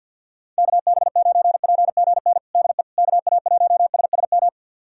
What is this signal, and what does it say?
Morse code, heard as a keyed tone, about 46 wpm, 700 Hz. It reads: KB9PGN DE KA1SSM
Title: USA Callsign DE Callsign - 6 Characters - 50wpm